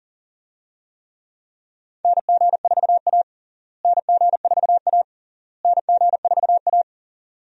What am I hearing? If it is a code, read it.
Morse code, heard as a keyed tone, about 40 wpm, 700 Hz. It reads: NG4A NG4A NG4A
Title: USA Call Signs - 4 Characters - Repeated 3x - 40wpm